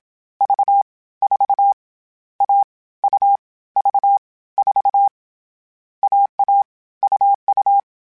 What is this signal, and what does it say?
Morse code, heard as a keyed tone, about 27 wpm, 800 Hz. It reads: V 4 A U V 4 AA UU
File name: Short Runs / Short Runs-mp3s / AUV AUV4 @25